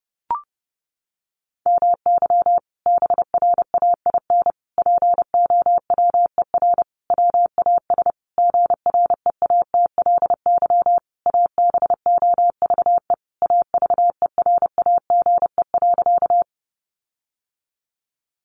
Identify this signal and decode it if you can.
Morse code, heard as a keyed tone, about 28 wpm, 700 Hz. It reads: MY BRAIN POWER WAS GREATLY ABOVE AVERAGE.
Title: Sentences from Top 1400 Words 30wpm